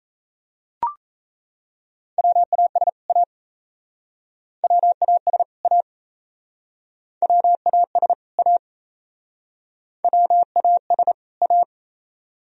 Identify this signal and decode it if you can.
Morse code, heard as a keyed tone, about 37 wpm, 700 Hz. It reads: WAS A WAS A WAS A WAS A